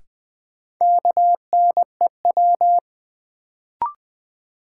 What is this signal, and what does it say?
Morse code, heard as a keyed tone, about 20 wpm, 700 Hz. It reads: KNEW
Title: Sentences from Top 400 Words - New Word Drill - 20wpm